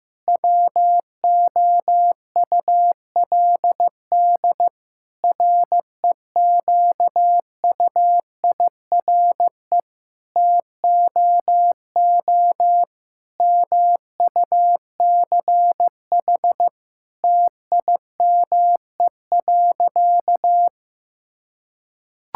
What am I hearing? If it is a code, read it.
Morse code, heard as a keyed tone, about 15 wpm, 700 Hz. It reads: WOULD REQUIRE TOO MUCH TIME.
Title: Sentences from Top 1000 Words - Review Part 6 - 15wpm